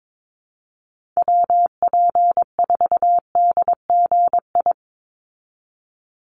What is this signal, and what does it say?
Morse code, heard as a keyed tone, about 22 wpm, 700 Hz. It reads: WP4DGI
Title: USA Call Signs 22wpm